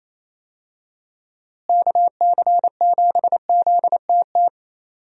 Morse code, heard as a keyed tone, about 28 wpm, 700 Hz. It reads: KC7ZTT
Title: USA Call Signs - 6 Characters - Repeated 3x - 28wpm